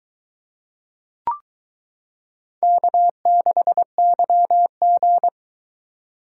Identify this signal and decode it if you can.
Morse code, heard as a keyed tone, about 23 wpm, 700 Hz. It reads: K6YG